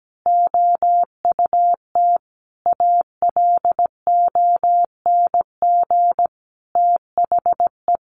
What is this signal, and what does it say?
Morse code, heard as a keyed tone, about 17 wpm, 700 Hz. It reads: OUT ALONG THE
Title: A Sound Of Thunder 17wpm